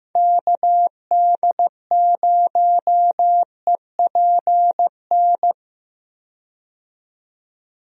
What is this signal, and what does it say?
Morse code, heard as a keyed tone, about 16 wpm, 700 Hz. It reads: KD0EPN
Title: USA Call Signs - 6 Characters - 15wpm